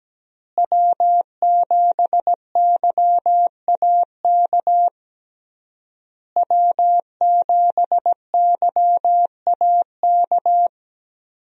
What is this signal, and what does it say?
Morse code, heard as a keyed tone, about 17 wpm, 700 Hz. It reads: W7YAK W7YAK
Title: USA Call Signs - 5 Characters - Repeated 3x - 17wpm